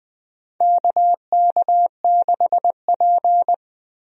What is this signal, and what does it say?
Morse code, heard as a keyed tone, about 20 wpm, 700 Hz. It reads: KK6P
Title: USA Call Signs - 4 Characters - Repeated 2x - 20wpm